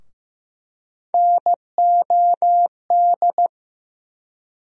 Morse code, heard as a keyed tone, about 15 wpm, 700 Hz. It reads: NOD
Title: USA Call Sign Postfixes - 3 Letters 15wpm